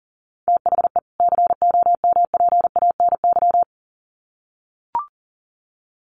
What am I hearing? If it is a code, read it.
Morse code, heard as a keyed tone, about 40 wpm, 700 Hz. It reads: THE COMPANY
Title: Sets of 2 Words 40wpm